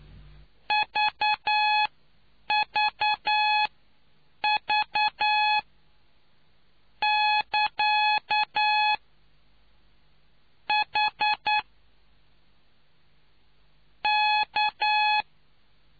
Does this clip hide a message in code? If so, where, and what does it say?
Yes: Morse code from 0.7 to 15.2 s, heard as a keyed tone, about 9 wpm, spaced as for 5 wpm, 850 Hz. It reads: VVV<KA>H K